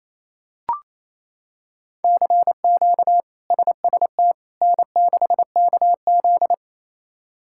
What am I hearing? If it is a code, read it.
Morse code, heard as a keyed tone, about 28 wpm, 700 Hz. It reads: CQ SST N6KZ
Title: K1USN SST QSO Practice - 28wpm